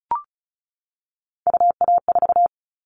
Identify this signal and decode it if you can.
Morse code, heard as a keyed tone, about 36 wpm, 700 Hz. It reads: UA4
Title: Callsign Prefixes for Top 50 DXCC - 35wpm